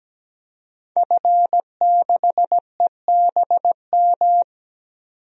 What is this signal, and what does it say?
Morse code, heard as a keyed tone, about 17 wpm, 700 Hz. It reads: F6EBM
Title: DXCC-100 Call Signs - 5 Characters - 17wpm